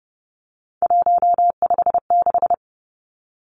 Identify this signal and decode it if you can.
Morse code, heard as a keyed tone, about 29 wpm, 700 Hz. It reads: 156
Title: Numbers - 3-Digit 30wpm